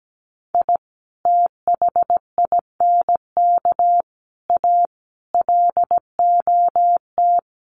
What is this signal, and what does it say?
Morse code, heard as a keyed tone, about 17 wpm, 700 Hz. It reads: I THINK A LOT
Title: Sets of 5 Words 17wpm